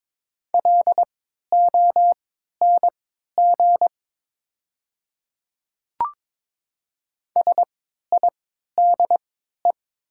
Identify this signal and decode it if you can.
Morse code, heard as a keyed tone, about 21 wpm, spaced as for 12 wpm, 700 Hz. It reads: LONG SIDE
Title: 3x Character Spacing - Top 100 Words - 22wpm